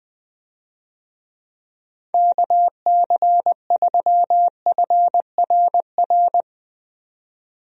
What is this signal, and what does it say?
Morse code, heard as a keyed tone, about 21 wpm, 700 Hz. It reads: KC3FRR
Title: USA Call Signs - 6 Characters - 20wpm